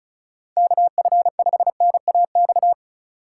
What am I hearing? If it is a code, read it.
Morse code, heard as a keyed tone, about 36 wpm, 700 Hz. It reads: KF5NAX
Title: USA Call Signs - 6 Characters - 3x Character Spacing - 35wpm